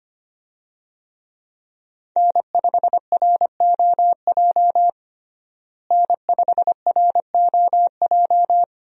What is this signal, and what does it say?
Morse code, heard as a keyed tone, about 25 wpm, 700 Hz. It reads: N5ROJ N5ROJ
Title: USA Call Signs - 5 Characters - Repeated 2x - 25wpm